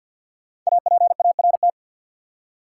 Morse code, heard as a keyed tone, about 50 wpm, 700 Hz. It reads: APART
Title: Five Letter Words - 50wpm